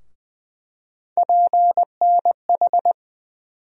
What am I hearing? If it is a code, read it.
Morse code, heard as a keyed tone, about 20 wpm, 700 Hz. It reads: PNH
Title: USA Call Sign Postfixes - 3 Letters 20wpm